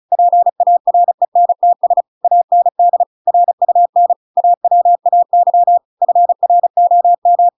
Morse code, heard as a keyed tone, about 34 wpm, 700 Hz. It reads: PARENTS AND RUN AWAY FROM